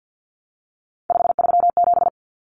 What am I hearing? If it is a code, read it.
Morse code, heard as a keyed tone, about 48 wpm, 700 Hz. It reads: 537
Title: Common Cut Numbers - 3-Digit 50wpm